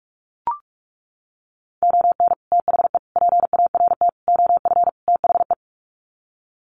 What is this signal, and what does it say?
Morse code, heard as a keyed tone, about 44 wpm, 700 Hz. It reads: ON THE PART OF THE
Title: Sets of 5 Words 45wpm